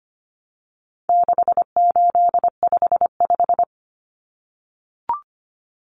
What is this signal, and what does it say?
Morse code, heard as a keyed tone, about 25 wpm, 700 Hz. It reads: 6855
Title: Common Cut Numbers - 4-Digit 25wpm